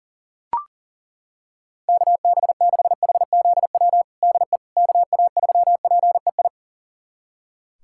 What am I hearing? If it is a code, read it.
Morse code, heard as a keyed tone, about 40 wpm, 700 Hz. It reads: KB6HZW DE KA3PEI
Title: USA Callsign DE Callsign - 6 Characters - 40wpm